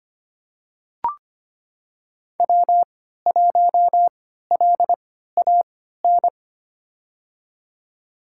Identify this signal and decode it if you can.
Morse code, heard as a keyed tone, about 25 wpm, spaced as for 14 wpm, 700 Hz. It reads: W1LAN
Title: USA Call Signs - 5 Characters - 3x Character Spacing - 25wpm